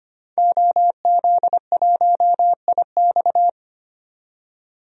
Morse code, heard as a keyed tone, about 25 wpm, 700 Hz. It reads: OZ1IX
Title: DXCC-100 Call Signs - 5 Characters - 25wpm